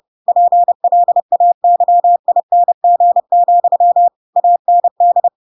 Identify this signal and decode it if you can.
Morse code, heard as a keyed tone, about 30 wpm, 700 Hz. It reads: PLAYING, AND
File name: Pinocchio / Pinocchio ch4 30 30 ik1hge